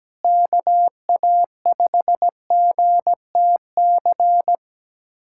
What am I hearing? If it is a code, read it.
Morse code, heard as a keyed tone, about 17 wpm, 700 Hz. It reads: KA5GTC
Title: USA Callsign DE Callsign - 6 Characters - 17wpm